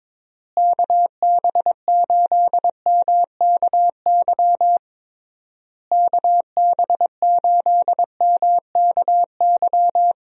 Morse code, heard as a keyed tone, about 22 wpm, 700 Hz. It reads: KB8MKY KB8MKY
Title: USA Call Signs - 6 Characters - Repeated 3x - 22wpm